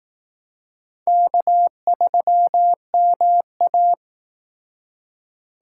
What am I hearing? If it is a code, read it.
Morse code, heard as a keyed tone, about 18 wpm, 700 Hz. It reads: K3MA
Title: Speed Racing - USA Call Signs 15wpm